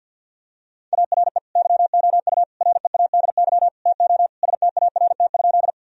Morse code, heard as a keyed tone, about 52 wpm, 700 Hz. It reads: ARE YOU READY TO START?
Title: Sentences from Top 1000 Words - Review Part 1 - 50wpm